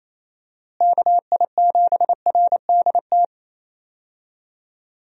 Morse code, heard as a keyed tone, about 28 wpm, 700 Hz. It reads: KI7RDT
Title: USA Call Signs - 6 Characters - Repeated 2x - 28wpm